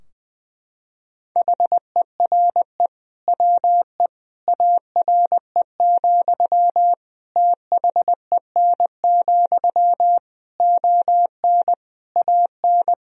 Morse code, heard as a keyed tone, about 20 wpm, 700 Hz. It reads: HERE WE ARE, THEN, ON AN
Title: Sentences from Top 1000 Words - Review Part 3 - 20wpm